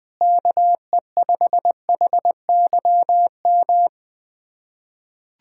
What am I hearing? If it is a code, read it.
Morse code, heard as a keyed tone, about 20 wpm, 700 Hz. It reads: KE5HYM
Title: USA Callsign DE Callsign - 6 Characters - 20wpm